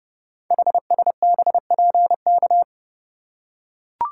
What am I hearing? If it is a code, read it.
Morse code, heard as a keyed tone, about 30 wpm, 700 Hz. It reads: HSBPK
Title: Code Groups - 5 Letters - 30wpm